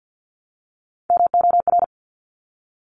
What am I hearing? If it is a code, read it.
Morse code, heard as a keyed tone, about 50 wpm, 700 Hz. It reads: MOR
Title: Top Three Letter Combinations in Words 50wpm